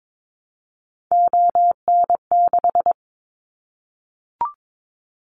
Morse code, heard as a keyed tone, about 22 wpm, 700 Hz. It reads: ON6
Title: Callsign Prefixes for Top 100 DXCC - 22wpm